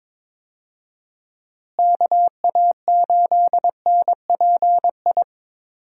Morse code, heard as a keyed tone, about 22 wpm, 700 Hz. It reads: KA8NPI
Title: USA Call Signs - 6 Characters - Repeated 2x - 22wpm